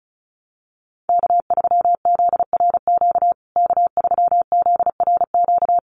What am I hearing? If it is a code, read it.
Morse code, heard as a keyed tone, about 34 wpm, 700 Hz. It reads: K3ZRQ K3ZRQ
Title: USA Call Signs Repeated 3x 35wpm